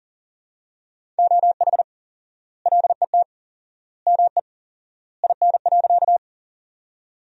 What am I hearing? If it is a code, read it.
Morse code, heard as a keyed tone, about 40 wpm, 700 Hz. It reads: OH LET ME IN.
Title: 4x Word-Spacing - Sentences from Top 500 Words - 40wpm